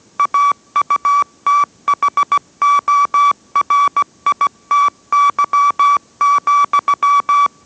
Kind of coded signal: Morse code